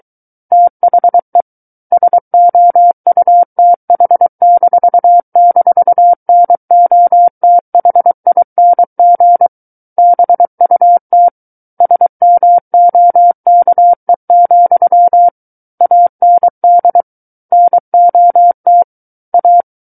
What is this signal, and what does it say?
Morse code, heard as a keyed tone, about 23 wpm, 700 Hz. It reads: THE SOUTH--NOTHING BUT SMOKE, AND NOT A
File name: Book One / 14-Chapter 14 - 23 WPM